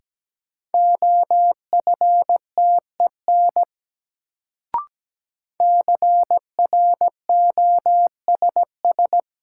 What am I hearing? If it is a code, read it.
Morse code, heard as a keyed tone, about 17 wpm, 700 Hz. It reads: OFTEN CROSS
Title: Sentences from Top 300 Words - New Word Drill - 17wpm